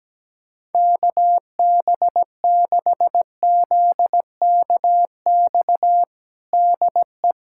Morse code, heard as a keyed tone, about 17 wpm, 700 Hz. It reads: KB6ZKX DE
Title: USA Callsign DE Callsign - 6 Characters - 17wpm